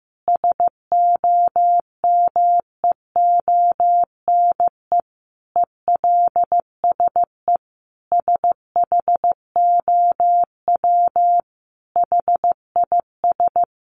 Morse code, heard as a keyed tone, about 15 wpm, 700 Hz. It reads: SOMEONE ELSE SHOW HIS
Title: Sentences from Top 1200 Words 15wpm